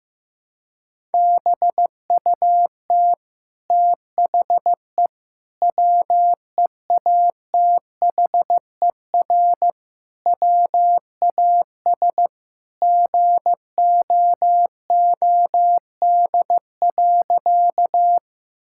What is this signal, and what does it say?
Morse code, heard as a keyed tone, about 15 wpm, 700 Hz. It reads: BUT THE WEATHER WAS GOOD.
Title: Sentences from Top 1000 Words - Review Part 5 - 15wpm